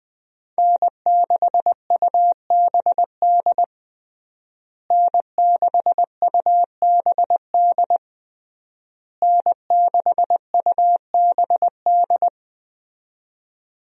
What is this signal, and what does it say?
Morse code, heard as a keyed tone, about 21 wpm, 700 Hz. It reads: N6UBD N6UBD N6UBD
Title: USA Call Signs - 5 Characters - Repeated 3x - 20wpm